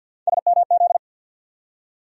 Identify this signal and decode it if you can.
Morse code, heard as a keyed tone, about 51 wpm, 700 Hz. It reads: SMZ
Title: USA Call Sign Postfixes - 3 Letters 50wpm